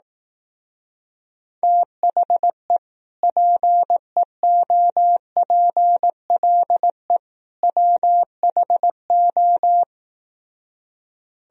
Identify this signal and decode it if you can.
Morse code, heard as a keyed tone, about 18 wpm, 700 Hz. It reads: THE PEOPLE WHO
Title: Speed-Racing - Sets of 3 Words 15wpm